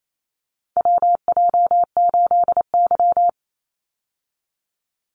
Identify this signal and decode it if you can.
Morse code, heard as a keyed tone, about 28 wpm, 700 Hz. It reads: WJ8Y
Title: Call Signs Repeated 2x - 28wpm